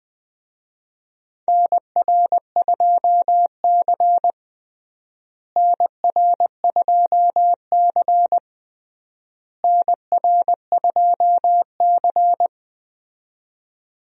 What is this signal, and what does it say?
Morse code, heard as a keyed tone, about 21 wpm, 700 Hz. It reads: NR2C NR2C NR2C
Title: USA Call Signs - 4 Characters - Repeated 3x - 20wpm